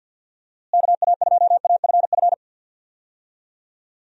Morse code, heard as a keyed tone, about 52 wpm, 700 Hz. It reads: KA2AFF
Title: USA Call Signs - 6 Characters - Repeated 3x - 50wpm